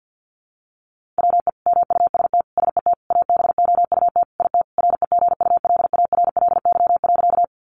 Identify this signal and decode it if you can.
Morse code, heard as a keyed tone, about 50 wpm, 700 Hz. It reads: WE MUST SET ABOUT IT REGULARLY.
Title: Sentences from Top 1400 Words 50wpm